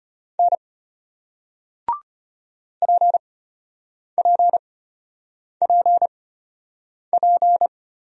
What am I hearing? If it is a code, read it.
Morse code, heard as a keyed tone, about 34 wpm, spaced as for 8 wpm, 700 Hz. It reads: N PPPP